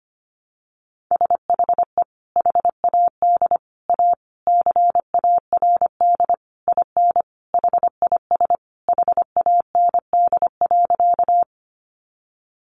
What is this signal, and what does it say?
Morse code, heard as a keyed tone, about 25 wpm, 700 Hz. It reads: SHE HAD A CARD IN HIS HAND.
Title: Sentences from Top 600 Words 25wpm